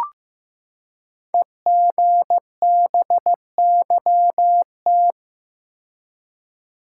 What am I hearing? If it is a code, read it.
Morse code, heard as a keyed tone, about 15 wpm, 700 Hz. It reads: EGBYT
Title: Code Groups - 5 Letter:Numbers - 15wpm